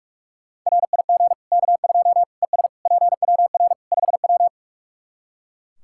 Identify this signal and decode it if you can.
Morse code, heard as a keyed tone, about 41 wpm, 700 Hz. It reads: RIG K2 ES PWR 5W
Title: Practice QSO - Exchange 2 - Info - 45wpm